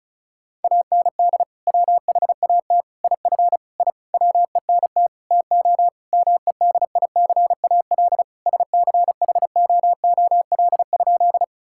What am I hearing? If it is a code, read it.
Morse code, heard as a keyed tone, about 34 wpm, 700 Hz. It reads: AND WHAT IF I WENT TO MEDICAL SCHOOL?
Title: Sentences from Top 1000 Words - Review Part 5 - 35wpm